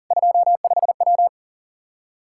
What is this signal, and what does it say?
Morse code, heard as a keyed tone, about 40 wpm, 700 Hz. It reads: 25W
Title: Practice QSO - Exchange 2 - Info - 40wpm